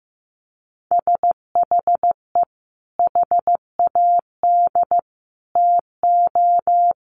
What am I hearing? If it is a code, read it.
Morse code, heard as a keyed tone, about 15 wpm, 700 Hz. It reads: SHE HAD TO
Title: Sentences from Top 200 Words 15wpm